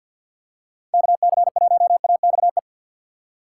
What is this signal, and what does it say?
Morse code, heard as a keyed tone, about 45 wpm, 700 Hz. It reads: KC1AXE
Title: USA Call Signs - 2x Character Spacing - 50wpm